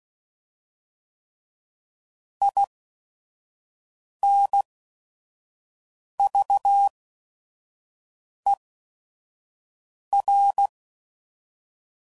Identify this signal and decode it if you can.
Morse code, heard as a keyed tone, about 16 wpm, spaced as for 5 wpm, 800 Hz. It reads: INVER
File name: CW Code Groups @5WPM / fromSM's / Antennas